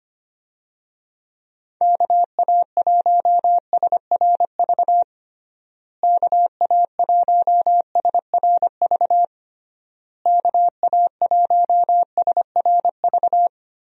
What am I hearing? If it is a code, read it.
Morse code, heard as a keyed tone, about 25 wpm, 700 Hz. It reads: KA1SRV KA1SRV KA1SRV
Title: USA Call Signs - 6 Characters - Repeated 3x - 25wpm